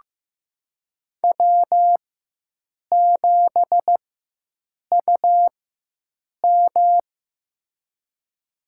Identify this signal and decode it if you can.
Morse code, heard as a keyed tone, about 15 wpm, spaced as for 7 wpm, 700 Hz. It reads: W7UM
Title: USA Call Signs - 4x Character Spacing - 15wpm